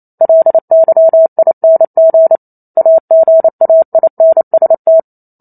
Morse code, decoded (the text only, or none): LYING AGAINST